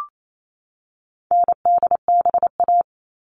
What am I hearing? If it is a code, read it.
Morse code, heard as a keyed tone, about 28 wpm, 700 Hz. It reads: NDBA
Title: All Cut Numbers - 4-Digit 28wpm